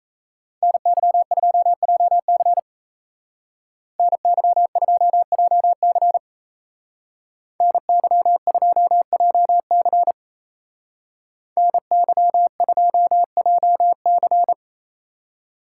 Morse code, heard as a keyed tone, about 36 wpm, 700 Hz. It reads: NY2JC NY2JC NY2JC NY2JC